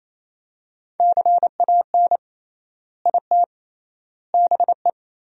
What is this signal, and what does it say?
Morse code, heard as a keyed tone, about 28 wpm, 700 Hz. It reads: CAN IT BE
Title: 3x Word-Spacing - Sentences from Top 400 Words - 28wpm